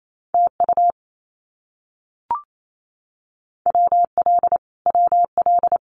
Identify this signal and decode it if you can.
Morse code, heard as a keyed tone, about 28 wpm, 700 Hz. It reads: TU WL WL
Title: Abbreviations Repeated 3x - 28wpm